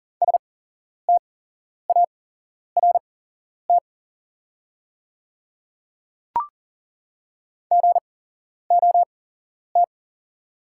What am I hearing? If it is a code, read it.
Morse code, heard as a keyed tone, about 40 wpm, spaced as for 11 wpm, 690 Hz. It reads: START GOT
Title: 8x Character Spacing - Top 500 Words - 40wpm